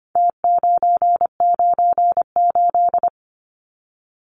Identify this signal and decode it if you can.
Morse code, heard as a keyed tone, about 25 wpm, 700 Hz. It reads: T998
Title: Numbers - Contest Style - 4-Digit 25wpm